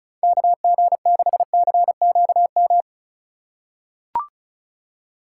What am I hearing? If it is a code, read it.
Morse code, heard as a keyed tone, about 36 wpm, 700 Hz. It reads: KG6CQM
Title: USA Call Signs - 3x Character Spacing - 35wpm